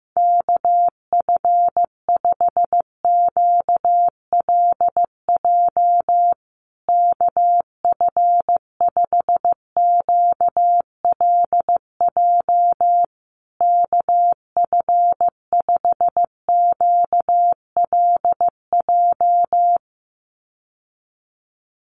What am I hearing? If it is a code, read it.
Morse code, heard as a keyed tone, about 14 wpm, 700 Hz. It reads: KF5QLJ KF5QLJ KF5QLJ
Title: USA Call Signs Repeated 3x 15wpm